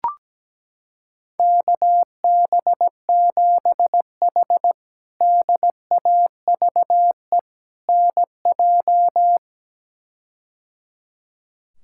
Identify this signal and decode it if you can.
Morse code, heard as a keyed tone, about 17 wpm, 700 Hz. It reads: KB7H DAVE NJ
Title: K1USN SST QSO Practice - 17wpm